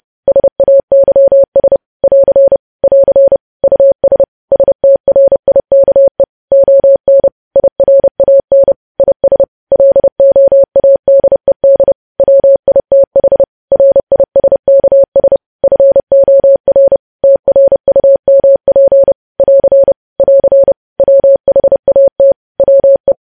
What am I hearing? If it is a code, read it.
Morse code, heard as a keyed tone, about 30 wpm, 550 Hz. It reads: SAYS <AR> <AR> US STRIKE ON IRAN IS LOADED WITH RISKS FOR TRUMP <AR> <AR> WHAT WE
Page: news headlines in morse code at 30 wpm